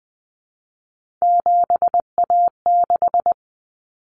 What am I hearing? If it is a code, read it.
Morse code, heard as a keyed tone, about 20 wpm, 700 Hz. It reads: 7A6
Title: Common Cut Numbers - 3-Digit 20wpm